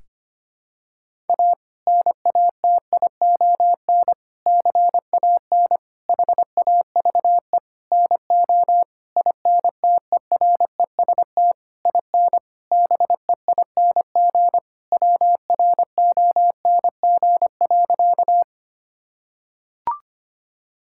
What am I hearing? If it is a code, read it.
Morse code, heard as a keyed tone, about 25 wpm, 700 Hz. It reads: A NATION CAN HAVE NO INTEREST IN BEING WRONG.
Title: Sentences from Top 1000 Words - Review Part 6 - 25wpm